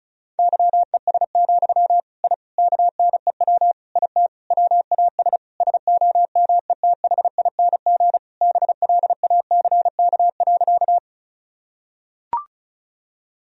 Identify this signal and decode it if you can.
Morse code, heard as a keyed tone, about 36 wpm, 700 Hz. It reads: YES, I KNEW IT WAS SOMETHING BLACK.
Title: Sentences from Top 1000 Words - Review Part 2 - 35wpm